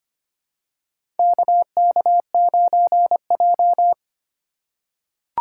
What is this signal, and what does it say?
Morse code, heard as a keyed tone, about 25 wpm, 700 Hz. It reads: KK9J
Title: USA Call Signs - 4 Characters - 4x Character Spacing - 25wpm